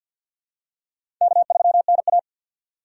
Morse code, heard as a keyed tone, about 51 wpm, 700 Hz. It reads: K3NA
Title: USA Call Signs - 4x Character Spacing - 50wpm